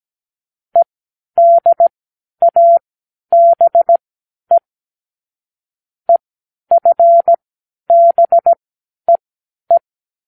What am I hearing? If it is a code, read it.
Morse code, heard as a keyed tone, about 17 wpm, spaced as for 11 wpm, 680 Hz. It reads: EDABE EFBEE
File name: k7qo / Lesson9(A-F)